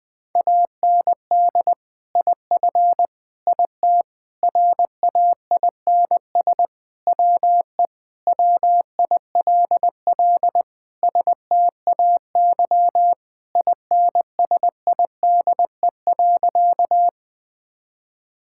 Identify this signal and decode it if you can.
Morse code, heard as a keyed tone, about 21 wpm, 700 Hz. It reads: AND IF IT RAINS WE WILL STAY INSIDE.
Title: Sentences with 2-Word Phrases - 20wpm